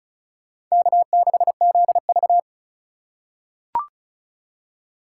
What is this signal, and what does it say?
Morse code, heard as a keyed tone, about 34 wpm, 700 Hz. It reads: K6ZV
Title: USA Call Signs - 4 Characters - 4x Character Spacing - 35wpm